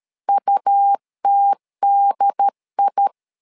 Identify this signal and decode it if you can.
Morse code, heard as a keyed tone, about 13 wpm, 800 Hz. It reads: UTDI
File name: LEZIONI / 3 MWRO / 3 M 12wpm 1541